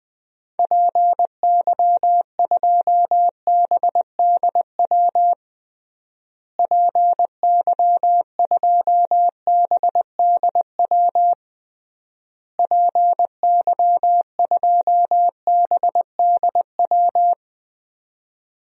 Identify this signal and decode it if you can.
Morse code, heard as a keyed tone, about 20 wpm, 700 Hz. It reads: PY2BDW PY2BDW PY2BDW
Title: DXCC-100 Call Signs - 6 Characters - Repeated 3x - 20wpm